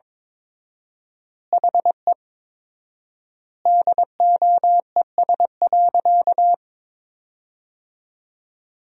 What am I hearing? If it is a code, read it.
Morse code, heard as a keyed tone, about 21 wpm, 700 Hz. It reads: HE DOES.
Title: 4x Word-Spacing - Sentences from Top 500 Words - 22wpm